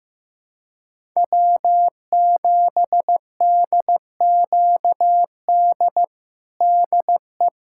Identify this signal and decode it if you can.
Morse code, heard as a keyed tone, about 15 wpm, 700 Hz. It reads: W7DQD DE
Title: USA Callsign DE Callsign - 5 Characters - 15wpm